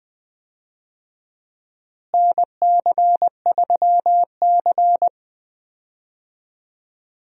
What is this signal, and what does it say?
Morse code, heard as a keyed tone, about 20 wpm, 700 Hz. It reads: NC3C
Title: USA Call Signs - 4 Characters - Repeated 3x - 20wpm